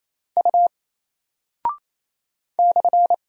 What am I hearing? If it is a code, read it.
Morse code, heard as a keyed tone, about 28 wpm, 700 Hz. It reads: U /
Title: Single Character (Letter:Number:Punctuation:Prosign) Alt 28wpm